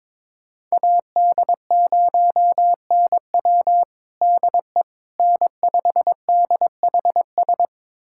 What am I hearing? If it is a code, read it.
Morse code, heard as a keyed tone, about 22 wpm, 700 Hz. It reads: AD0NW DE N5DHS
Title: USA Callsign DE Callsign - 5 Characters - 22wpm